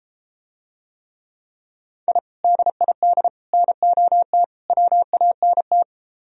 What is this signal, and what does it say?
Morse code, heard as a keyed tone, about 33 wpm, 700 Hz. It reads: I DID NOT WANT